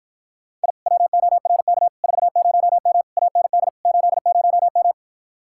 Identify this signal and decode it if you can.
Morse code, heard as a keyed tone, about 54 wpm, 700 Hz. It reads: I WORK 40M AND 80M